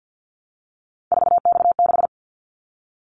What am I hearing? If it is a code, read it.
Morse code, heard as a keyed tone, about 50 wpm, 700 Hz. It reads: 4X6
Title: Callsign Prefixes for Top 50 DXCC - 50wpm